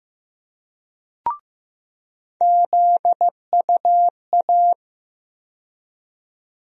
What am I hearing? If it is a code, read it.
Morse code, heard as a keyed tone, about 15 wpm, 700 Hz. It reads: ZUA
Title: USA Call Sign Postfixes - 3 Letters 15wpm